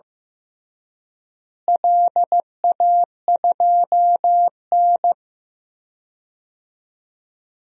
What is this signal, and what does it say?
Morse code, heard as a keyed tone, about 15 wpm, 700 Hz. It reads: LA2N
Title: DXCC-100 Call Signs - 4 Characters - Repeated 2x - 15wpm